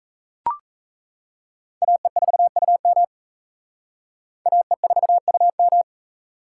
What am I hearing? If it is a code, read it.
Morse code, heard as a keyed tone, about 37 wpm, 700 Hz. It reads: AE4UM AE4UM